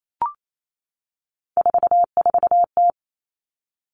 Morse code, heard as a keyed tone, about 28 wpm, 700 Hz. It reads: 44T
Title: Three Challenging Letter-Number Combinations 28wpm